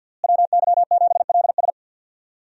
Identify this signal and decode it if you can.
Morse code, heard as a keyed tone, about 46 wpm, 700 Hz. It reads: WY7LS
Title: USA Call Signs - 5 Characters - Repeated 2x - 50wpm